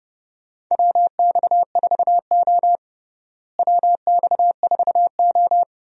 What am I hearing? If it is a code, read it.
Morse code, heard as a keyed tone, about 30 wpm, 700 Hz. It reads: WX4O WX4O
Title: USA Call Signs - 4 Characters - Repeated 2x - 30wpm